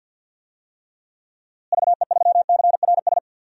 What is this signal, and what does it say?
Morse code, heard as a keyed tone, about 51 wpm, 700 Hz. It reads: VE3CRS
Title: DXCC-100 Call Signs - 6 Characters - Repeated 2x - 50wpm